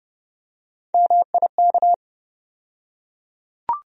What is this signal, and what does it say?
Morse code, heard as a keyed tone, about 32 wpm, 700 Hz. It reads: MIK
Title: USA Call Sign Postfixes - 3 Letters 30wpm